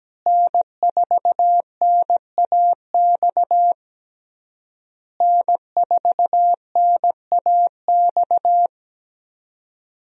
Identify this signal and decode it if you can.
Morse code, heard as a keyed tone, about 17 wpm, 700 Hz. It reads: N4NAX N4NAX
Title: USA Call Signs - 5 Characters - Repeated 3x - 17wpm